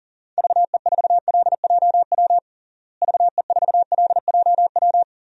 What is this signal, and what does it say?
Morse code, heard as a keyed tone, about 40 wpm, 700 Hz. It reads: VE4LJW VE4LJW
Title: DXCC-100 Call Signs - 6 Characters - Repeated 2x - 40wpm